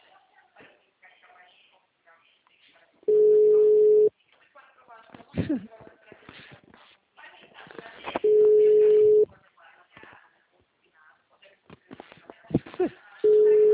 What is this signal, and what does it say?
Ringback tone (Europe, 425 Hz), with speech.
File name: out- -2012- - - .98779